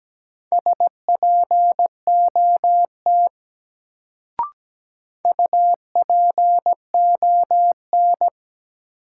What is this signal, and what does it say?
Morse code, heard as a keyed tone, about 17 wpm, 700 Hz. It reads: SPOT UPON
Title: Sentences from Top 1100 Words - New Word Drill Alt - 17wpm